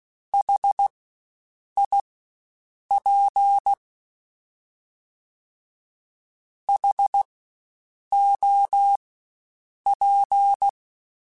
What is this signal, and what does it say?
Morse code, heard as a keyed tone, about 16 wpm, spaced as for 7 wpm, 800 Hz. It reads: HIP HOP
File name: Spaced Alphas / Spaced Alphas mp3s / Hs sp @15